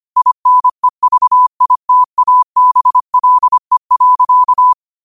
Morse code, heard as a keyed tone, about 25 wpm, 1000 Hz. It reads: INEVITABLE.
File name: qotdfiles / 25wpm